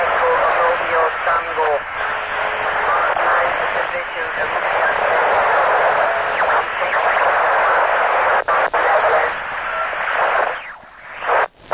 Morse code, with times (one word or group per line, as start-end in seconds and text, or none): none